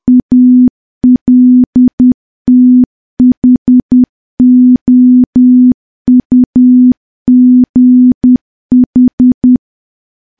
Morse code, heard as a keyed tone, about 10 wpm, 250 Hz. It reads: ALTHOUGH
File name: morse / morse-training-10WPM